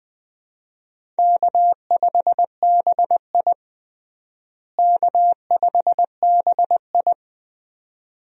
Morse code, heard as a keyed tone, about 21 wpm, 700 Hz. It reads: K5BI K5BI
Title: USA Call Signs - 4 Characters - Repeated 2x - 20wpm